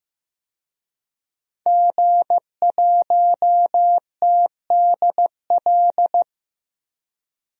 Morse code, heard as a keyed tone, about 16 wpm, 700 Hz. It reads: G1TDL
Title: DXCC-100 Call Signs - 5 Characters - 15wpm